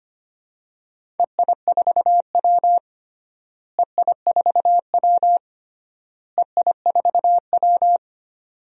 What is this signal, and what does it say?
Morse code, heard as a keyed tone, about 25 wpm, 700 Hz. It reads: EI4W EI4W EI4W
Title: DXCC-100 Call Signs - 4 Characters - Repeated 3x - 25wpm